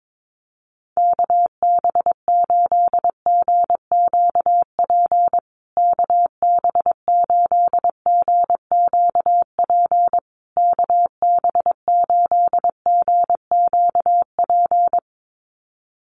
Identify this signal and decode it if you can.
Morse code, heard as a keyed tone, about 22 wpm, 700 Hz. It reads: KB8GQP KB8GQP KB8GQP
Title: USA Call Signs Repeated 3x 22wpm